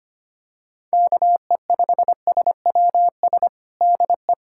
Morse code, heard as a keyed tone, about 25 wpm, 700 Hz. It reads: KE5SWS DE
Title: USA Callsign DE Callsign - 6 Characters - 25wpm